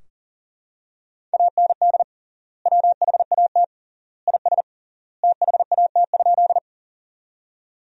Morse code, heard as a keyed tone, about 40 wpm, 700 Hz. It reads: AND WHAT IS THAT?
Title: 3x Word-Spacing - Sentences from Top 100 Words - 40wpm